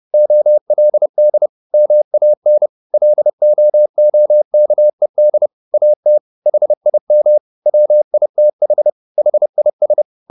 Morse code, heard as a keyed tone, about 30 wpm, 600 Hz. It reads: OLD MAN LOOKED AT HIM WITH HIS
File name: the old man and the sea / the old man and the sea 1 30wpm